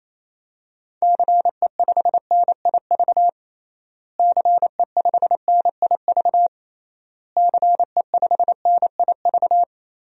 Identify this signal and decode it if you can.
Morse code, heard as a keyed tone, about 28 wpm, 700 Hz. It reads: CE5NIV CE5NIV CE5NIV
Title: DXCC-100 Call Signs - 6 Characters - Repeated 3x - 28wpm